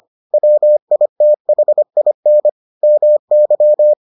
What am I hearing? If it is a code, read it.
Morse code, heard as a keyed tone, about 25 wpm, 600 Hz. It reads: WITHIN MY